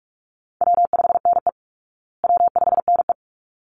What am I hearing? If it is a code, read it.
Morse code, heard as a keyed tone, about 44 wpm, 700 Hz. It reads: W5NE W5NE
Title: Easy Call Signs Repeated 2x - 45wpm